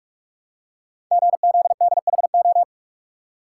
Morse code, heard as a keyed tone, about 44 wpm, 700 Hz. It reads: GZDHO
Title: Code Groups - 5 Letters - 45wpm